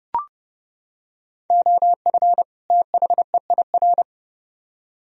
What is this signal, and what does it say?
Morse code, heard as a keyed tone, about 29 wpm, 700 Hz. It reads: OF THEIR